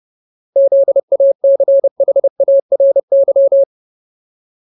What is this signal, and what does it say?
Morse code, heard as a keyed tone, about 30 wpm, 550 Hz. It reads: ZACHARY
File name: Ultimate Word List / 30WPM Ultimate